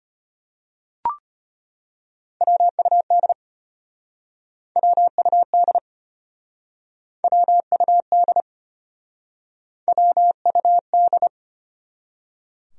Morse code, heard as a keyed tone, about 36 wpm, 700 Hz. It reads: WUD WUD WUD WUD